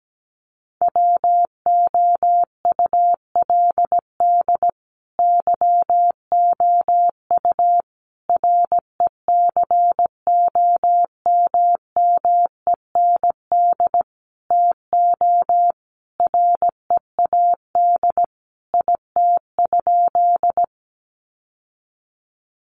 Morse code, heard as a keyed tone, about 17 wpm, 700 Hz. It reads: WOULD YOU RECOMMEND TO READ IT?
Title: Sentences from Top 1200 Words 17wpm